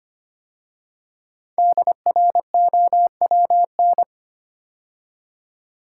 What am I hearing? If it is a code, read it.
Morse code, heard as a keyed tone, about 25 wpm, 700 Hz. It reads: DROWN
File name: 5 Letter Words - First 3 Letters Given - 25wpm